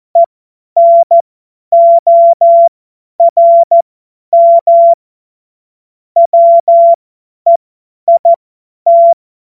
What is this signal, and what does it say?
Morse code, heard as a keyed tone, about 14 wpm, spaced as for 10 wpm, 680 Hz. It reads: ENORM WEIT
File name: Morsekurs / Witze / Witz 28.November 2023 50 F70